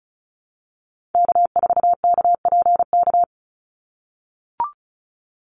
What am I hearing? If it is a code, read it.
Morse code, heard as a keyed tone, about 34 wpm, 700 Hz. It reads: K4KPK
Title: NA SOTA Activator Callsigns 35wpm